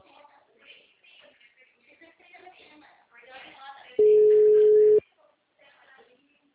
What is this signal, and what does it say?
Ringback tone (Europe, 425 Hz), with speech.